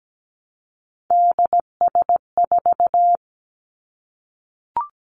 Morse code, heard as a keyed tone, about 17 wpm, 700 Hz. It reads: DS4
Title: Three Challenging Letter-Number Combinations 17wpm